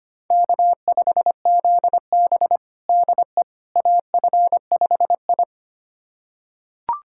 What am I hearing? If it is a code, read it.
Morse code, heard as a keyed tone, about 25 wpm, 700 Hz. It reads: K5ZB DE AF5I
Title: USA Callsign DE Callsign - 4 Characters - 25wpm